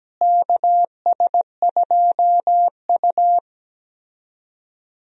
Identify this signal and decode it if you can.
Morse code, heard as a keyed tone, about 17 wpm, 700 Hz. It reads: KS2U
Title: USA Call Signs - 4 Characters - Repeated 2x - 17wpm